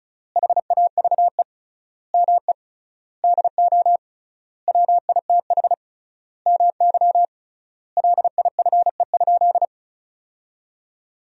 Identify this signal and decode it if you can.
Morse code, heard as a keyed tone, about 36 wpm, 700 Hz. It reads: HAVE ME DO WITH MY LIFE?
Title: 3x Word-Spacing - Sentences from Top 300 Words - 35wpm